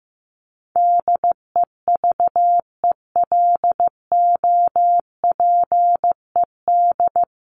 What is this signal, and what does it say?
Morse code, heard as a keyed tone, about 14 wpm, 700 Hz. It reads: DEVELOPED
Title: ED Suffix - Top 100 Words with Base Words 15wpm